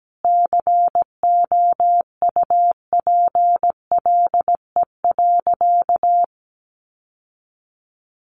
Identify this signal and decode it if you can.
Morse code, heard as a keyed tone, about 17 wpm, 700 Hz. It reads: COUPLE.
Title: Sentences from Top 1100 Words 17wpm